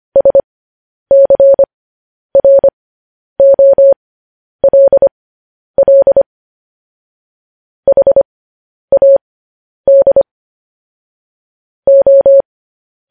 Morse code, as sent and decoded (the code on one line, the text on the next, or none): ... -.-. .-. --- .-.. .-.. / .... .- -.. / ---
SCROLL HAD O